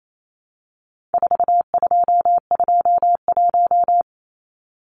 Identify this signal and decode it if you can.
Morse code, heard as a keyed tone, about 28 wpm, 700 Hz. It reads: 4221
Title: Numbers - 4-Digit 28wpm